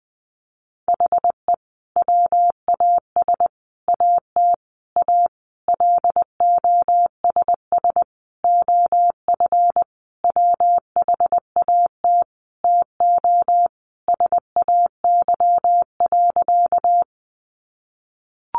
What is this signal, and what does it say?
Morse code, heard as a keyed tone, about 19 wpm, 700 Hz. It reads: HE WAS AT A LOSS OF WHAT TO SAY.
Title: Sentences from Top 1400 Words 20wpm